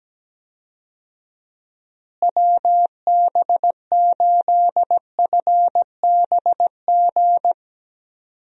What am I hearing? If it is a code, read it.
Morse code, heard as a keyed tone, about 17 wpm, 700 Hz. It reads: WB8FBG
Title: USA Call Signs - 6 Characters - Repeated 2x - 17wpm